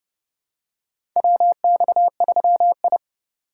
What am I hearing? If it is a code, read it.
Morse code, heard as a keyed tone, about 32 wpm, 700 Hz. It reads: WX3I
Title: USA Call Signs - 4x Character Spacing - 30wpm